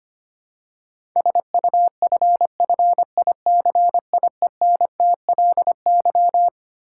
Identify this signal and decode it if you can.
Morse code, heard as a keyed tone, about 25 wpm, 700 Hz. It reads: SUFFICIENTLY
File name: Twelve Letter Words - 25wpm